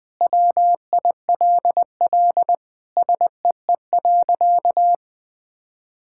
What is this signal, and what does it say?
Morse code, heard as a keyed tone, about 20 wpm, 700 Hz. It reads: WILL SEE.
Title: Sentences with 3-Word Phrases - 20wpm